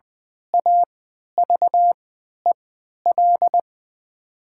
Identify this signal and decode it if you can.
Morse code, heard as a keyed tone, about 20 wpm, spaced as for 11 wpm, 700 Hz. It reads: AVEL
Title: 3x Character Spacing - Top 500 Words - 20wpm